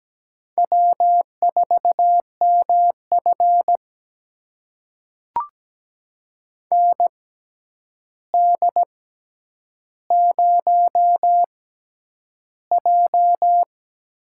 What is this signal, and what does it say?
Morse code, heard as a keyed tone, about 17 wpm, spaced as for 9 wpm, 700 Hz. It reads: W4MF ND0J
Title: USA Call Signs - 4 Characters - 6x Character Spacing - 17wpm